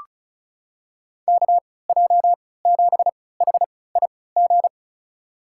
Morse code, heard as a keyed tone, about 36 wpm, spaced as for 20 wpm, 700 Hz. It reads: KJ7HIG
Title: USA Call Signs - 6 Characters - 3x Character Spacing - 35wpm